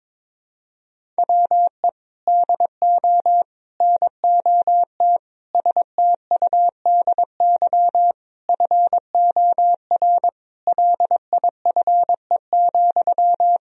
Morse code, heard as a keyed tone, about 22 wpm, 700 Hz. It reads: WE DO NOT STUDY FOR LIFE,
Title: Sentences from Top 1000 Words - Review Part 1 - 22wpm